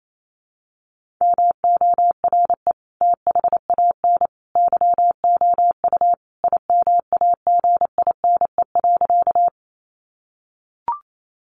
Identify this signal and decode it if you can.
Morse code, heard as a keyed tone, about 28 wpm, 700 Hz. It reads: MORE THAN YOU IMAGINE.
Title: Sentences from Top 1400 Words 28wpm